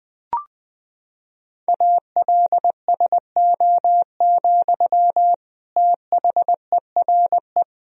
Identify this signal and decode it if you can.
Morse code, heard as a keyed tone, about 20 wpm, 700 Hz. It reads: ALSO, THERE
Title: Sentences from Top 1000 Words - Review Part 1 - 20wpm